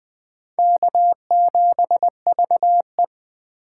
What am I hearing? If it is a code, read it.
Morse code, heard as a keyed tone, about 20 wpm, 700 Hz. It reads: K7VE
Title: USA Call Signs - 3x Character Spacing - 20wpm